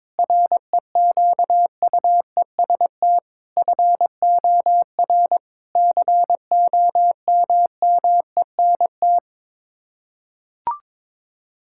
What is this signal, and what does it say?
Morse code, heard as a keyed tone, about 22 wpm, 700 Hz. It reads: REQUEST FOR COMMENT
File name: practice / 22wpm / Sets of 8 Words - 22wpm